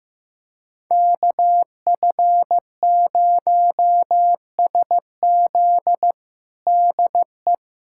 Morse code, heard as a keyed tone, about 15 wpm, 700 Hz. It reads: KF0SZ DE
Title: USA Callsign DE Callsign - 5 Characters - 15wpm